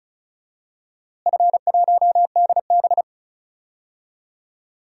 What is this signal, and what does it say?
Morse code, heard as a keyed tone, about 34 wpm, 700 Hz. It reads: F1DB
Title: DXCC-100 Call Signs - 4 Characters - Repeated 2x - 35wpm